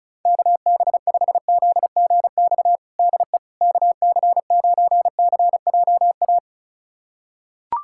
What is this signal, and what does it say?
Morse code, heard as a keyed tone, about 34 wpm, 700 Hz. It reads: KB5ZGX DE KC9CJA
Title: USA Callsign DE Callsign - 6 Characters - 35wpm